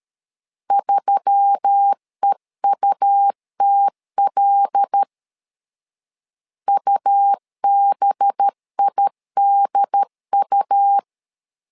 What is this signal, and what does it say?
Morse code, heard as a keyed tone, about 13 wpm, 800 Hz. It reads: 3EUTL UBIDU
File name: LEZIONI / 5 LBVC / 5 LB 12wpm 1906